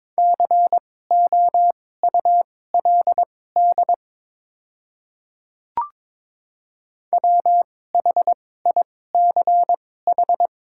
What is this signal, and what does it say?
Morse code, heard as a keyed tone, about 22 wpm, spaced as for 16 wpm, 700 Hz. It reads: COULD WHICH
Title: 2x Character Spacing - Top 200 Words - 22wpm